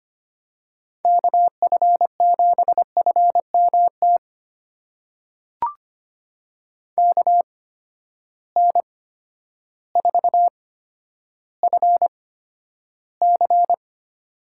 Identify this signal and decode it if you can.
Morse code, heard as a keyed tone, about 25 wpm, spaced as for 11 wpm, 700 Hz. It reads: KF7FMT KN4FC
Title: USA Call Signs - 8x Character Spacing - 25wpm